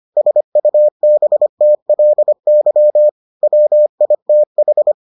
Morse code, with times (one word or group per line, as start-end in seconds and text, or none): 0.2-3.1 SUBTLY
3.4-4.9 WITH